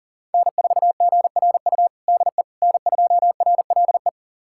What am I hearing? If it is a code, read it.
Morse code, heard as a keyed tone, about 40 wpm, 700 Hz. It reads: N4GRU DE N2RLE
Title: USA Callsign DE Callsign - 5 Characters - 40wpm